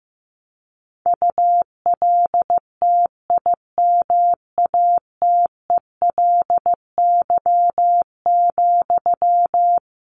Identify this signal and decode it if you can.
Morse code, heard as a keyed tone, about 15 wpm, 700 Hz. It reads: ULTIMATELY,
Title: Sentences from Top 1200 Words 15wpm